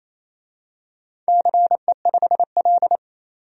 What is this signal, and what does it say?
Morse code, heard as a keyed tone, about 28 wpm, 700 Hz. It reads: CE5L
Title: DXCC-100 Call Signs - 4 Characters - 28wpm